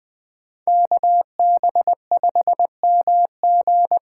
Morse code, heard as a keyed tone, about 20 wpm, 700 Hz. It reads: KB5MG
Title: USA Call Signs - 6 Characters - 8x Character Spacing - 20wpm